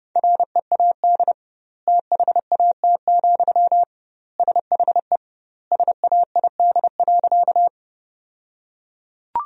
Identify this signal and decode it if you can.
Morse code, heard as a keyed tone, about 30 wpm, 700 Hz. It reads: READ THAT, SHE SAID.
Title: 2x Word-Spacing - Sentences from Top 200 Words - 30wpm